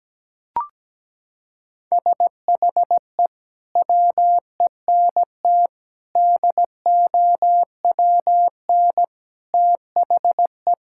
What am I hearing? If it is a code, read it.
Morse code, heard as a keyed tone, about 17 wpm, 700 Hz. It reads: SHE WENT DOWN THE
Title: Sentences from Top 1000 Words - Review Part 5 - 17wpm